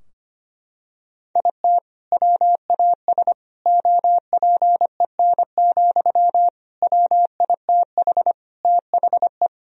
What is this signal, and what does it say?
Morse code, heard as a keyed tone, about 25 wpm, 700 Hz. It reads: IT WAS OPEN, WITH THE
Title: Sentences from Top 1000 Words - Review Part 4 - 25wpm